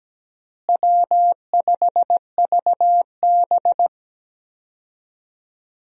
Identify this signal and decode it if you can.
Morse code, heard as a keyed tone, about 17 wpm, 700 Hz. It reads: W5VB
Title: USA Call Signs - 4 Characters - Repeated 2x - 17wpm